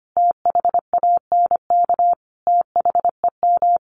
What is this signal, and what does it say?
Morse code, heard as a keyed tone, about 25 wpm, 700 Hz. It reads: THANK THEM
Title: Sentences from Top 1400 Words 25wpm